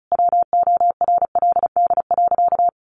Morse code, heard as a keyed tone, about 34 wpm, 700 Hz. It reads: WORLD.
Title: Sentences from Top 1200 Words 35wpm